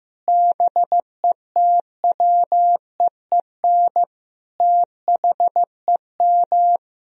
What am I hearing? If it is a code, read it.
Morse code, heard as a keyed tone, about 15 wpm, 700 Hz. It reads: BETWEEN THEM
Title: Sentences from Top 1000 Words - Review Part 1 - 15wpm